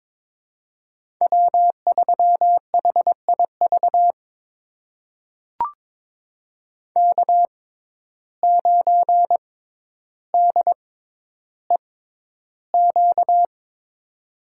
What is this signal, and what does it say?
Morse code, heard as a keyed tone, about 22 wpm, spaced as for 11 wpm, 700 Hz. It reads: W3HIV K9DEQ
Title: USA Call Signs - 5 Characters - 6x Character Spacing - 22wpm